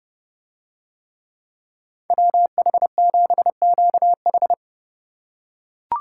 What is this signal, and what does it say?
Morse code, heard as a keyed tone, about 29 wpm, 700 Hz. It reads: WH7QH